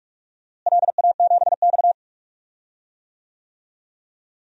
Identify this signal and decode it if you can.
Morse code, heard as a keyed tone, about 46 wpm, 700 Hz. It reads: LA7X
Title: DXCC-100 Call Signs - 4 Characters - Repeated 3x - 45wpm